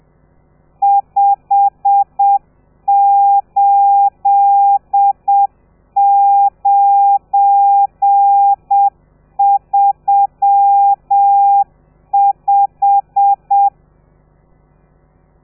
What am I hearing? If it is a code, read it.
Morse code, heard as a keyed tone, about 7 wpm, 800 Hz. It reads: 58935
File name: telegrafia / i2tfj / Lez12 30